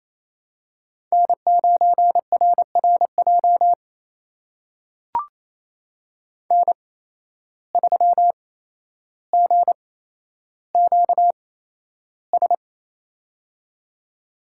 Morse code, heard as a keyed tone, about 28 wpm, spaced as for 12 wpm, 700 Hz. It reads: N9RRJ N3GQS
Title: USA Call Signs - 5 Characters - 8x Character Spacing - 28wpm